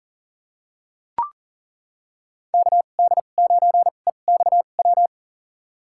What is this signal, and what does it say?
Morse code, heard as a keyed tone, about 40 wpm, spaced as for 29 wpm, 700 Hz. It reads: KD9EXW
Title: USA Call Signs - 6 Characters - 2x Character Spacing - 40wpm